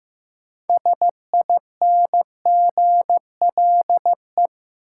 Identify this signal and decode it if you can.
Morse code, heard as a keyed tone, about 15 wpm, 700 Hz. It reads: SINGLE
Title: Sentences from Top 700 Words - New Word Drill - 15wpm